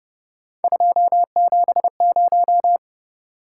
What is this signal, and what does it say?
Morse code, heard as a keyed tone, about 29 wpm, 700 Hz. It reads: 270
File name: Speed-Racing - Three Letter-Number Combinations - 25wpm